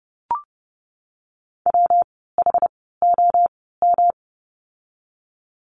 Morse code, heard as a keyed tone, about 30 wpm, spaced as for 17 wpm, 700 Hz. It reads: WHOM
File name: course / 30:10wpm / 115 Lesson 19f - Letter w - Words 30wpm